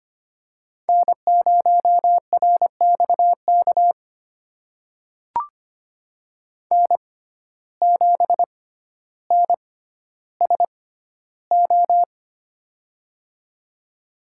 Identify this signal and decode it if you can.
Morse code, heard as a keyed tone, about 25 wpm, spaced as for 13 wpm, 700 Hz. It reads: N0RXK N7NSO
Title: USA Call Signs - 5 Characters - 6x Character Spacing - 25wpm